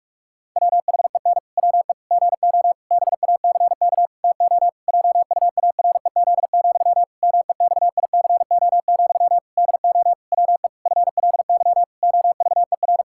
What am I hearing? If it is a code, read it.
Morse code, heard as a keyed tone, about 41 wpm, 700 Hz. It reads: WHEN WE GO BACK TO JUAREZ, MEXICO, DO WE FLY OVER
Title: Pangram Challenge v2 - 45wpm